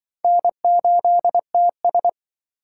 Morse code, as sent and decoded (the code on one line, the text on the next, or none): -. ---.. - ...
N8TS